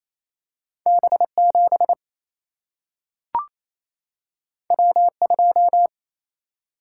Morse code, heard as a keyed tone, about 28 wpm, 700 Hz. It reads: B7 W2
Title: Single-Letter Single-Number 28wpm